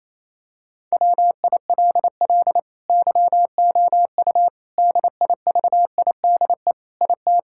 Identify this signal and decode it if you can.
Morse code, heard as a keyed tone, about 28 wpm, 700 Hz. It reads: WILL YOU DIVIDE IT
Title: Sentences from Top 1600 Words 28wpm